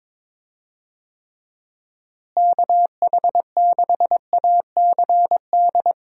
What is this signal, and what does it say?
Morse code, heard as a keyed tone, about 22 wpm, 700 Hz. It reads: KH6ACD
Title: USA Call Signs - 6 Characters - Repeated 3x - 22wpm